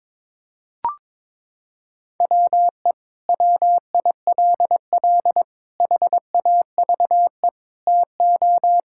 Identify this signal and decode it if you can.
Morse code, heard as a keyed tone, about 22 wpm, 700 Hz. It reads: WE WILL HAVE TO
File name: practice / 22wpm / Sentences with 3-Word Phrases - 22wpm